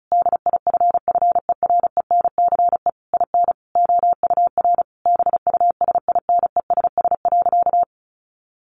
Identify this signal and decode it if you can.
Morse code, heard as a keyed tone, about 34 wpm, 700 Hz. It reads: DIFFERENCE IN OUR BUSINESS.
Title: Sentences from Top 1400 Words 35wpm